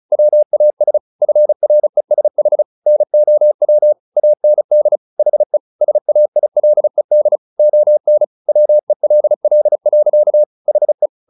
Morse code, heard as a keyed tone, about 34 wpm, 600 Hz. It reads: WAS FRESH NOW AND HE SAILED ON WELL. HE
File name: the old man and the sea 2 35wpm